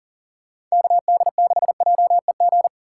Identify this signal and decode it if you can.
Morse code, heard as a keyed tone, about 40 wpm, 700 Hz. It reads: KD6JEG
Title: USA Call Signs - 6 Characters - 4x Character Spacing - 40wpm